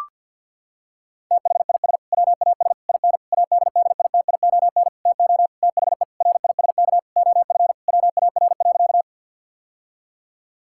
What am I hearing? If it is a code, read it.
Morse code, heard as a keyed tone, about 50 wpm, 700 Hz. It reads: THIS WAS IN ADDITION TO THE RISK OF WAR.
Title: Sentences from Top 1000 Words - Review Part 6 - 50wpm